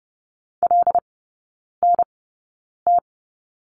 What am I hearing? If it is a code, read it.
Morse code, heard as a keyed tone, about 30 wpm, spaced as for 9 wpm, 700 Hz. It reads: LNT
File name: course / 30:10wpm / 087 Lesson 15e - Number 5 - Three Characters 30wpm